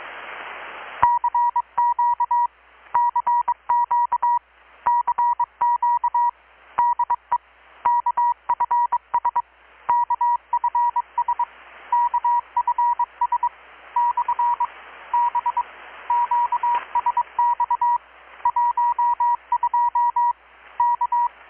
Morse code, heard as a keyed tone, about 22 wpm, 960 Hz. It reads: CQ CQ CQ DE KFS KFS KFS / B QSX 12 K
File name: blog / swl / 6-abril-2019-12695-USB-KPS